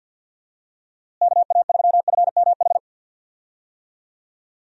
Morse code, heard as a keyed tone, about 47 wpm, 700 Hz. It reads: KA3FMH
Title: USA Call Signs - 6 Characters - Repeated 2x - 50wpm